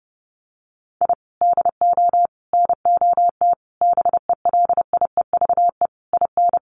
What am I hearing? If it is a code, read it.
Morse code, heard as a keyed tone, about 28 wpm, 700 Hz. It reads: I DO NOT BELIEVE IN
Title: Sets of 5 Words Encore 30wpm